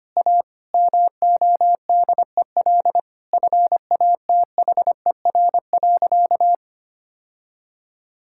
Morse code, heard as a keyed tone, about 25 wpm, 700 Hz. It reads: A MODEL FATHER.
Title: Sentences from Top 1000 Words - Review Part 5 - 25wpm